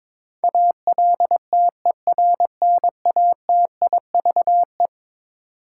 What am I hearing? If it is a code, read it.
Morse code, heard as a keyed tone, about 21 wpm, 700 Hz. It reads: ALTERNATIVE
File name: practice / 22wpm / Eleven Letter Words - 22wpm